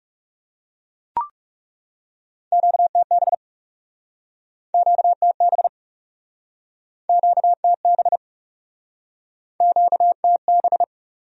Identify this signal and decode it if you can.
Morse code, heard as a keyed tone, about 42 wpm, 700 Hz. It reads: QTB QTB QTB QTB